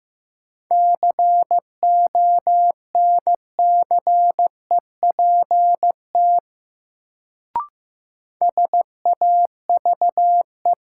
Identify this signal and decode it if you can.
Morse code, heard as a keyed tone, about 15 wpm, 700 Hz. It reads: CONCEPT SAVE
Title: Sentences from Top 1100 Words - New Word Drill - 15wpm